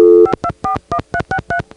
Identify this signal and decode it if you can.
DTMF, heard as telephone touch tones, tones of about 80 ms. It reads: 5241363